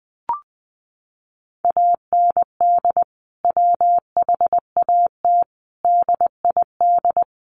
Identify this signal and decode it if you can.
Morse code, heard as a keyed tone, about 20 wpm, 700 Hz. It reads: AND WHAT DID
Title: Sentences from Top 100 Words 20wpm